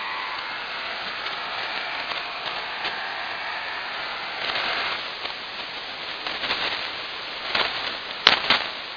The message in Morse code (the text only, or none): none